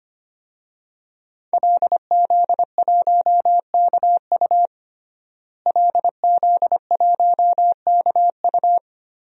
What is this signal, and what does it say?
Morse code, heard as a keyed tone, about 25 wpm, 700 Hz. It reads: LZ1KU LZ1KU
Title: DXCC-100 Call Signs - 5 Characters - Repeated 3x - 25wpm